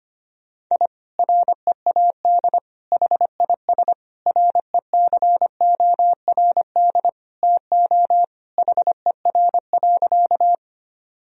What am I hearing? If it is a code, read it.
Morse code, heard as a keyed tone, about 25 wpm, 700 Hz. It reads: I READ HIS RECORD TO HER.
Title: Sentences from Top 1000 Words - Review Part 3 - 25wpm